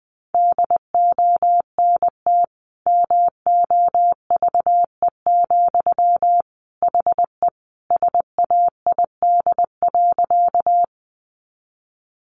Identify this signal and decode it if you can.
Morse code, heard as a keyed tone, about 20 wpm, 700 Hz. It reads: DONT MOVE, HE SAID.
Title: Sentences from Top 300 Words 20wpm